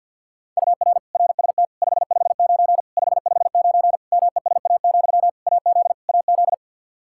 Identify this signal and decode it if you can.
Morse code, heard as a keyed tone, about 52 wpm, 700 Hz. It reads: UR RST 559 559 MESA, AZ AZ
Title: Practice QSO - Exchange 1 - Beginning - 50wpm